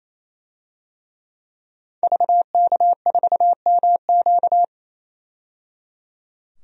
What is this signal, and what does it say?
Morse code, heard as a keyed tone, about 28 wpm, 690 Hz. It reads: VK4MQ
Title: DXCC-100 Call Signs - 5 Characters - 28wpm